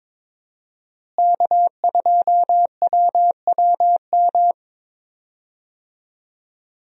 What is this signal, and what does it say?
Morse code, heard as a keyed tone, about 22 wpm, 700 Hz. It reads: K2WWM
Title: USA Call Signs - 5 Characters - Repeated 2x - 22wpm